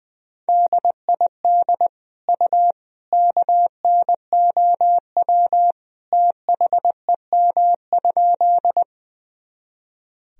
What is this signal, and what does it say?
Morse code, heard as a keyed tone, about 21 wpm, 700 Hz. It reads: DID U KNOW THEM?
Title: Sentences from Top 100 Words and Abbreviations 20wpm